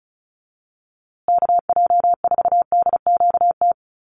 Morse code, heard as a keyed tone, about 34 wpm, 700 Hz. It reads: KJ4DQT
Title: USA Call Signs Repeated 3x 35wpm